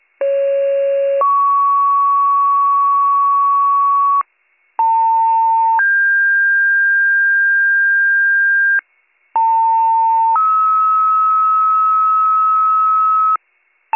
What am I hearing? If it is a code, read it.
Two-tone page, heard as a one-second tone, then a three-second tone, about 550 then 1120 Hz, about 900 then 1600 Hz, about 900 then 1290 Hz, with noise. It is heard from 0.2 to 13.4 s.